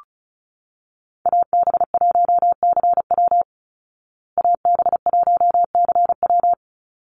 Morse code, heard as a keyed tone, about 34 wpm, 700 Hz. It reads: AB1CW AB1CW
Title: Call Signs Repeated 2x - 35wpm